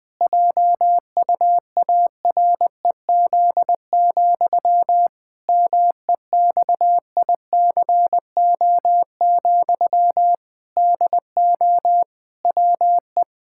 Morse code, heard as a keyed tone, about 20 wpm, 700 Hz. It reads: JUAREZ, MEXICO, DO WE
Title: Pangram Challenge v2 - 20wpm